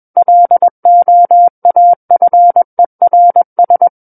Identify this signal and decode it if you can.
Morse code, heard as a keyed tone, about 21 wpm, 700 Hz. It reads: LOAFERS
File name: Book One / 12-Chapter 12 - 21 WPM